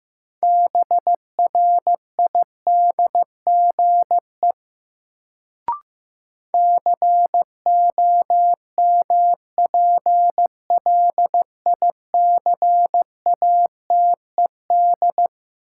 Morse code, heard as a keyed tone, about 15 wpm, 700 Hz. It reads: BRIDGE COMPLICATED
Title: Sentences from Top 1200 Words - New Word Drill Alt - 15wpm